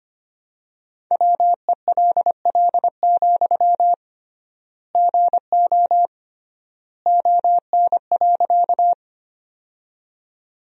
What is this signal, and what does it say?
Morse code, heard as a keyed tone, about 25 wpm, 700 Hz. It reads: WELL, GO ON.
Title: 3x Word-Spacing - Sentences from Top 200 Words - 25wpm